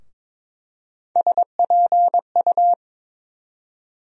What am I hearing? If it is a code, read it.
Morse code, heard as a keyed tone, about 22 wpm, 700 Hz. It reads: SPU
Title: USA Call Sign Postfixes - 3 Letters 22wpm